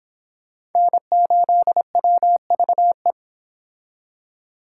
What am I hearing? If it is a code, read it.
Morse code, heard as a keyed tone, about 26 wpm, 700 Hz. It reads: N8WVE